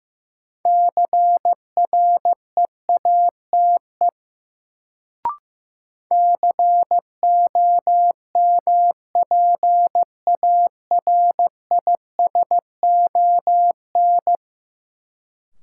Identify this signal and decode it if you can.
Morse code, heard as a keyed tone, about 15 wpm, 700 Hz. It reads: CREATE COMPARISON
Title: Sentences from Top 1600 Words - New Word Drill Alt - 15wpm